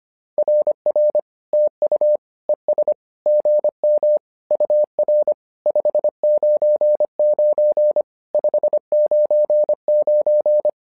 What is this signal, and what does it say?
Morse code, heard as a keyed tone, about 25 wpm, 600 Hz. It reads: RR TU ES GM UR 599 599
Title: SOTA QSO Practice 25wpm